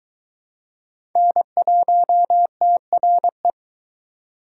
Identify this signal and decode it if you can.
Morse code, heard as a keyed tone, about 21 wpm, 700 Hz. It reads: N1TRE